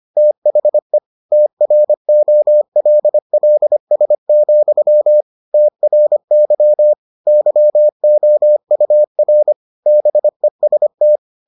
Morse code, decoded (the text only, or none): THE TROLLS, TRY YOUR BEST